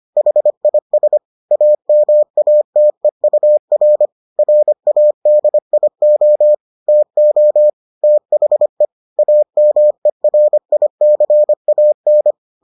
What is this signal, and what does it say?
Morse code, heard as a keyed tone, about 25 wpm, 600 Hz. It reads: HIS AMATEUR RADIO TO THE AMERICAN